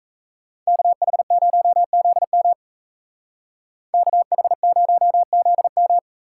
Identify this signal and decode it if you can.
Morse code, heard as a keyed tone, about 32 wpm, 700 Hz. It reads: KH0ZM KH0ZM
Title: Speed Racing - USA 2x2 Call Signs 28wpm